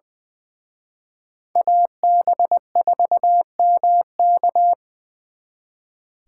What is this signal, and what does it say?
Morse code, heard as a keyed tone, about 23 wpm, 700 Hz. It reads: AB4MK
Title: Speed Racing - USA Call Signs 20wpm